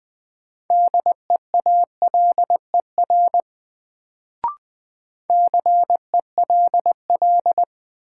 Morse code, heard as a keyed tone, about 20 wpm, 700 Hz. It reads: DEALER CELL
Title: Sentences from Top 1100 Words - New Word Drill Alt - 20wpm